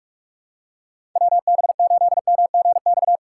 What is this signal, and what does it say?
Morse code, heard as a keyed tone, about 46 wpm, 700 Hz. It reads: WB8MGX
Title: USA Call Signs - 6 Characters - 2x Character Spacing - 45wpm